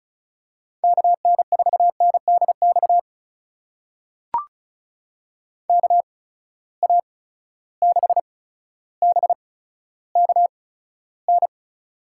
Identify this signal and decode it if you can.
Morse code, heard as a keyed tone, about 36 wpm, spaced as for 15 wpm, 700 Hz. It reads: KN4NDX KA6BKN
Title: USA Call Signs - 8x Character Spacing - 35wpm